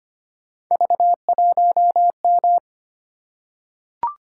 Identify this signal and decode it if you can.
Morse code, heard as a keyed tone, about 25 wpm, 700 Hz. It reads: V1M
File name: Three Letter-Number Combinations - 25wpm